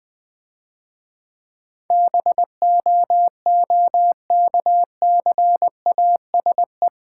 Morse code, heard as a keyed tone, about 22 wpm, 700 Hz. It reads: BOOKCASE
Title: Difficult Words - Speed-Racing with Hints - 20wpm